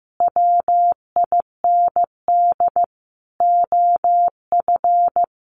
Morse code, heard as a keyed tone, about 14 wpm, 700 Hz. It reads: WIND OF
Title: A Sound Of Thunder 15wpm